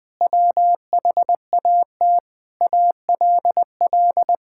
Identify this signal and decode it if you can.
Morse code, heard as a keyed tone, about 20 wpm, 700 Hz. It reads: WHAT ALL
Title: Sentences with 8-Word Phrases - 20wpm